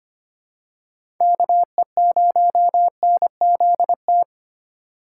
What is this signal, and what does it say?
Morse code, heard as a keyed tone, about 25 wpm, 700 Hz. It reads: KE0NZT
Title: USA Call Signs - 6 Characters - 2x Character Spacing - 25wpm